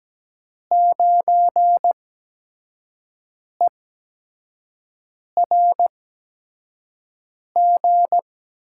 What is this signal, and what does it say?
Morse code, heard as a keyed tone, about 17 wpm, spaced as for 5 wpm, 700 Hz. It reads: 9ERG
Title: USA Call Signs - 6 Characters - 8x Character Spacing - 17wpm